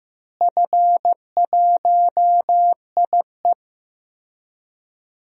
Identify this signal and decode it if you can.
Morse code, heard as a keyed tone, about 15 wpm, 700 Hz. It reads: F1IE
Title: DXCC-100 Call Signs - 4 Characters - Repeated 3x - 15wpm